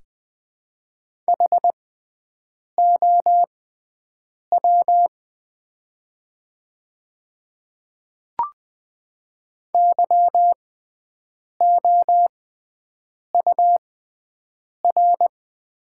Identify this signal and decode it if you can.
Morse code, heard as a keyed tone, about 20 wpm, spaced as for 7 wpm, 700 Hz. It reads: HOW YOUR
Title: 6x Character Spacing - Top 100 Words - 20wpm